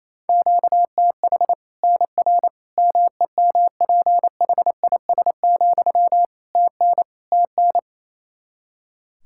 Morse code, heard as a keyed tone, about 28 wpm, 700 Hz. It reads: QTH NR MEMPHIS, TN TN
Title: Practice QSO - Exchange 1 - Beginning - 28wpm